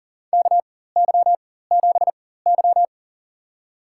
Morse code, heard as a keyed tone, about 40 wpm, spaced as for 19 wpm, 700 Hz. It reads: KY7Y
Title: USA Call Signs - 4 Characters - 4x Character Spacing - 40wpm